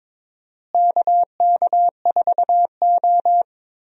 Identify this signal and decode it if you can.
Morse code, heard as a keyed tone, about 22 wpm, 700 Hz. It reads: KK4O
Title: USA Call Signs - 4 Characters - Repeated 2x - 22wpm